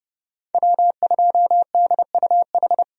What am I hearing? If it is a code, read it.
Morse code, heard as a keyed tone, about 31 wpm, 700 Hz. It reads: W2DUH
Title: USA Call Signs - 5 Characters - 30wpm